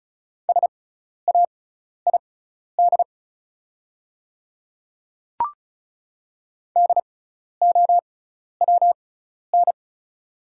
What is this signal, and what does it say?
Morse code, heard as a keyed tone, about 36 wpm, spaced as for 12 wpm, 700 Hz. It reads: SAID DOWN
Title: 6x Character Spacing - Top 100 Words - 35wpm